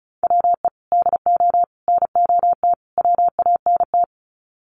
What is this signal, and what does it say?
Morse code, heard as a keyed tone, about 34 wpm, 700 Hz. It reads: WE DO NOT WANT
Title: Sets of 4 Words 35wpm